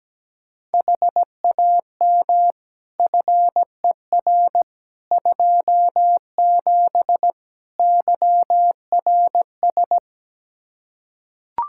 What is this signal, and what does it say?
Morse code, heard as a keyed tone, about 17 wpm, 700 Hz. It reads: HAM FER 27 YRS
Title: Practice QSO - Exchange 3 - More Info - 17wpm